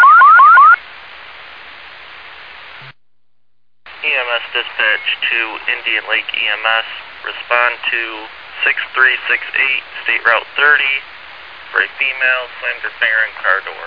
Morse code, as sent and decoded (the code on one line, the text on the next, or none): none